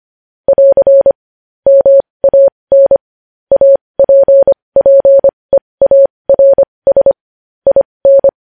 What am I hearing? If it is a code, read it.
Morse code, heard as a keyed tone, about 25 wpm, spaced as for 20 wpm, 550 Hz. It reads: <AR> MAN APPEARS IN
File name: podcast / News headlines-1x-25x20